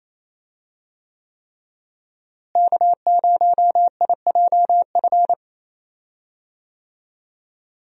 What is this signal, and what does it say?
Morse code, heard as a keyed tone, about 28 wpm, 700 Hz. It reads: K0IJF
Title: USA Call Signs - 5 Characters - Repeated 3x - 28wpm